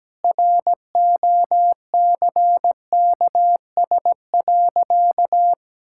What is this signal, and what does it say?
Morse code, heard as a keyed tone, about 17 wpm, 700 Hz. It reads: ROCKS.
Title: Pangram Challenge v2 - 17wpm